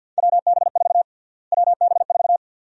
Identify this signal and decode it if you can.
Morse code, heard as a keyed tone, about 53 wpm, 700 Hz. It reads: WB4 WB4
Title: USA Call Sign Prefixes - Repeated 2x - 50wpm